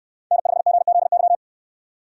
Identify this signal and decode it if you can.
Morse code, heard as a keyed tone, about 68 wpm, 700 Hz. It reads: N5GZY